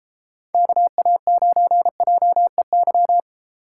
Morse code, heard as a keyed tone, about 32 wpm, 700 Hz. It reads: KA9JEY